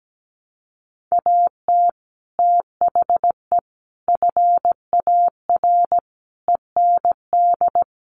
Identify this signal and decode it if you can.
Morse code, heard as a keyed tone, about 17 wpm, 700 Hz. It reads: AT THE FAR END
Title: Sets of 5 Words 17wpm